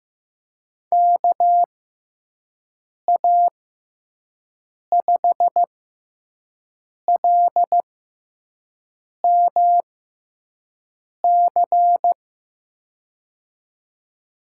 Morse code, heard as a keyed tone, about 15 wpm, spaced as for 5 wpm, 700 Hz. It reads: KA5LMC
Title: USA Call Signs - 6 Characters - 6x Character Spacing - 15wpm